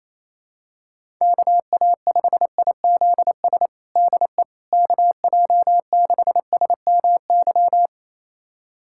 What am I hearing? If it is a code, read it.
Morse code, heard as a keyed tone, about 28 wpm, 700 Hz. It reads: KA5IZS DE KJ6SMY
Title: USA Callsign DE Callsign - 6 Characters - 28wpm